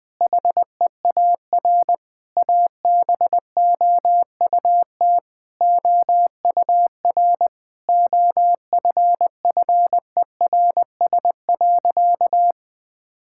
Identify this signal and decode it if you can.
Morse code, heard as a keyed tone, about 20 wpm, 700 Hz. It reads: HEAR ABOUT OUR OFFERS.
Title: Sentences with 6-Word Phrases - Spoken Beginning - 20wpm